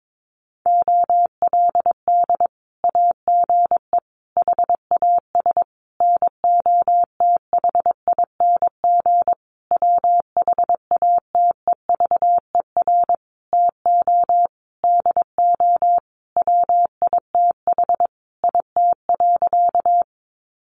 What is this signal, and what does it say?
Morse code, heard as a keyed tone, about 22 wpm, 700 Hz. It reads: OLD AGE HAS NOTHING WHATEVER TO DO WITH IT.
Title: Sentences from Top 1100 Words 22wpm